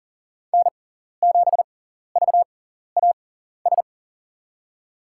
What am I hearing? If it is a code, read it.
Morse code, heard as a keyed tone, about 39 wpm, spaced as for 14 wpm, 700 Hz. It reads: N7VAS
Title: USA Call Signs - 5 Characters - 6x Character Spacing - 40wpm